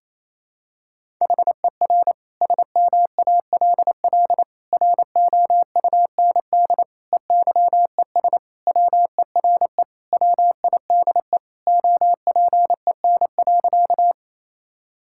Morse code, heard as a keyed tone, about 27 wpm, 700 Hz. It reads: HER SMALL ROUND EYES WERE WIDE OPEN.
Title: Sentences from Top 1000 Words - Review Part 6 - 28wpm